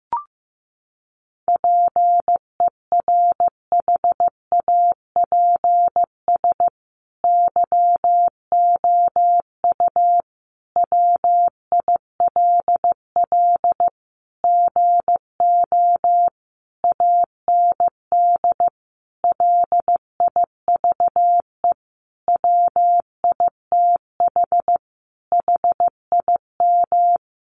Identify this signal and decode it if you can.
Morse code, heard as a keyed tone, about 15 wpm, 700 Hz. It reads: PERHAPS YOU WILL GO AND LIVE WITH HIM
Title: Sentences from Top 500 Words 15wpm